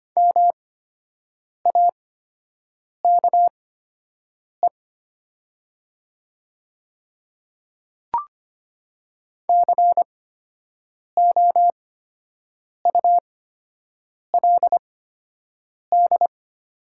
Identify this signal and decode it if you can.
Morse code, heard as a keyed tone, about 25 wpm, spaced as for 7 wpm, 700 Hz. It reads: MAKE COULD
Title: 8x Character Spacing - Top 500 Words - 25wpm